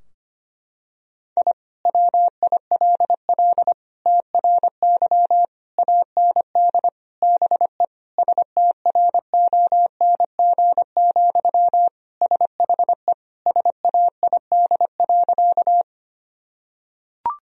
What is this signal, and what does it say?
Morse code, heard as a keyed tone, about 25 wpm, 700 Hz. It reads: I WILL TRY AND BE STRONG, SHE SAID.
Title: Sentences from Top 1000 Words - Review Part 3 - 25wpm